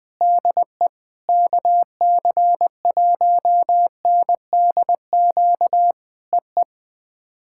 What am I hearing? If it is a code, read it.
Morse code, heard as a keyed tone, about 20 wpm, 700 Hz. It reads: DE KC1NDQ EE
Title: POTA QSO Practice - 20wpm